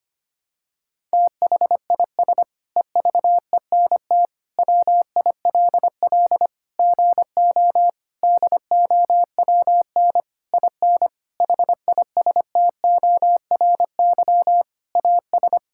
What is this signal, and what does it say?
Morse code, heard as a keyed tone, about 25 wpm, 700 Hz. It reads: THIS EVENT WILL GO DOWN IN HISTORY AS
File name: practice / 25wpm / Sentences with 9-Word Phrases - 25wpm